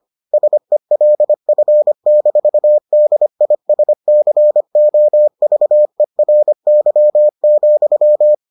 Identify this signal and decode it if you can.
Morse code, heard as a keyed tone, about 25 wpm, 600 Hz. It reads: SELF-DISCOVERY,